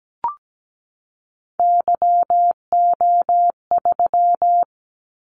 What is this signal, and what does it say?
Morse code, heard as a keyed tone, about 17 wpm, 700 Hz. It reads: YO3
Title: Callsign Prefixes for Top 25 DXCC - 17wpm